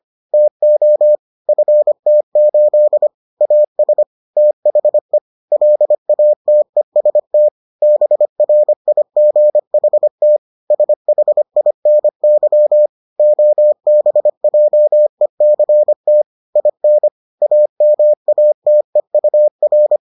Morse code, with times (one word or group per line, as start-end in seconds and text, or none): 0.3-1.2 TO
1.5-3.1 FT8
3.4-4.0 AS
4.4-5.2 THE
5.5-7.5 LATEST
7.8-10.4 BRIGHT
10.7-12.9 SHINY
13.2-16.2 OBJECT
16.6-17.1 IN
17.4-20.0 AMATEUR